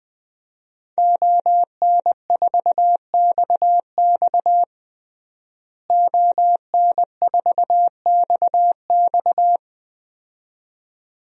Morse code, heard as a keyed tone, about 21 wpm, 700 Hz. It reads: ON4XX ON4XX
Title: DXCC-100 Call Signs - 5 Characters - Repeated 2x - 20wpm